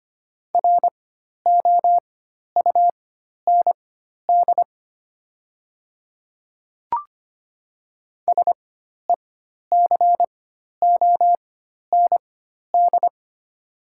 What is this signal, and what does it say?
Morse code, heard as a keyed tone, about 25 wpm, spaced as for 12 wpm, 690 Hz. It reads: ROUND SECOND
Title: 4x Character Spacing - Top 300 Words - 25wpm